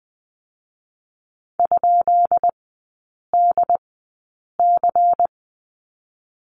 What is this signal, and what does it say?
Morse code, heard as a keyed tone, about 20 wpm, 700 Hz. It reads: ? D C
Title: 133 Lesson 22e - Question Mark - Three Characters 20wpm